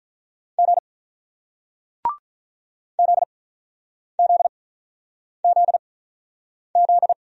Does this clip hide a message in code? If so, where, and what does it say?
Yes: Morse code from 0.6 to 7.1 s, heard as a keyed tone, about 48 wpm, spaced as for 9 wpm, 700 Hz. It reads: G ZZZZ